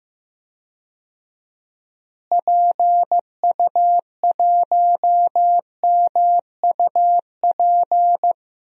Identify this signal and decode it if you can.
Morse code, heard as a keyed tone, about 16 wpm, 700 Hz. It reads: PU1MUP
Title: DXCC-100 Call Signs - 6 Characters - Repeated 2x - 15wpm